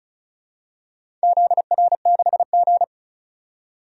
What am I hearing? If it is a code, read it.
Morse code, heard as a keyed tone, about 36 wpm, 700 Hz. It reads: ZR6G
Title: DXCC-100 Call Signs - 4 Characters - 35wpm